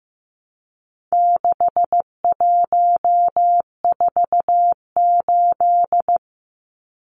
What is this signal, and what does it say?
Morse code, heard as a keyed tone, about 15 wpm, 700 Hz. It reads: 6148
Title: Numbers - 4-Digit 15wpm